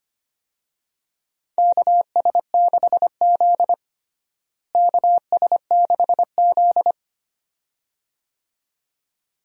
Morse code, heard as a keyed tone, about 25 wpm, 700 Hz. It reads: KS6Z KS6Z
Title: USA Call Signs - 4 Characters - Repeated 2x - 25wpm